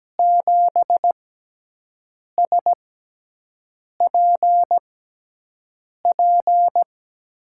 Morse code, heard as a keyed tone, about 17 wpm, spaced as for 6 wpm, 700 Hz. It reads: 7SPP
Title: USA Call Signs - 6x Character Spacing - 17wpm